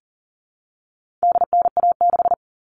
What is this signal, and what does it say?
Morse code, heard as a keyed tone, about 40 wpm, 700 Hz. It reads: DNA6
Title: All Cut Numbers - 4-Digit 40wpm